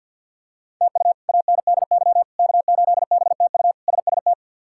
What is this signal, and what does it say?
Morse code, heard as a keyed tone, about 53 wpm, 690 Hz. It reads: TU ANDY K8BTU SST
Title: K1USN SST QSO Practice - 50wpm